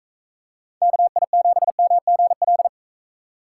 Morse code, heard as a keyed tone, about 41 wpm, 700 Hz. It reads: KI7MGL